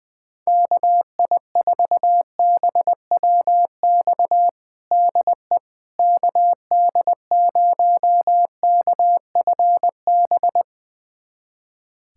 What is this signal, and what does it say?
Morse code, heard as a keyed tone, about 20 wpm, 700 Hz. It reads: KI4BWX DE KD0KFB
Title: USA Callsign DE Callsign - 6 Characters - 20wpm